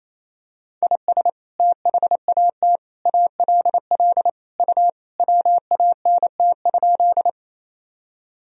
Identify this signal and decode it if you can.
Morse code, heard as a keyed tone, about 28 wpm, 700 Hz. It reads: IS THAT ALL U WANT?
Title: Sentences from Top 200 Words and Abbreviations 28wpm